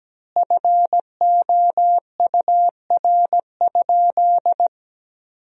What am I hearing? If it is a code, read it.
Morse code, heard as a keyed tone, about 17 wpm, 700 Hz. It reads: FOUR?
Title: 2x Word-Spacing - Sentences from Top 300 Words - 17wpm